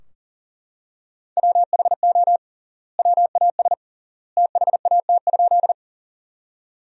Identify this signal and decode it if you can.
Morse code, heard as a keyed tone, about 40 wpm, 700 Hz. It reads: WHO WAS THAT?
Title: 3x Word-Spacing - Sentences from Top 100 Words - 40wpm